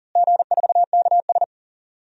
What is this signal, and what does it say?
Morse code, heard as a keyed tone, about 40 wpm, 700 Hz. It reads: G4KS
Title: DXCC-100 Call Signs - 4 Characters - Repeated 2x - 40wpm